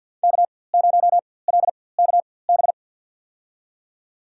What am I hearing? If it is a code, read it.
Morse code, heard as a keyed tone, about 47 wpm, spaced as for 23 wpm, 700 Hz. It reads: K0LKB
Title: USA Call Signs - 5 Characters - 4x Character Spacing - 50wpm